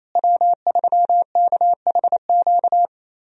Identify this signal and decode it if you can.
Morse code, heard as a keyed tone, about 28 wpm, 700 Hz. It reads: W3KHQ
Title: USA Call Signs - 5 Characters - Repeated 2x - 28wpm